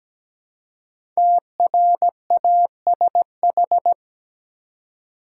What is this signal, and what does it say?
Morse code, heard as a keyed tone, about 17 wpm, 700 Hz. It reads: TRASH
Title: Five Letter Words - 17wpm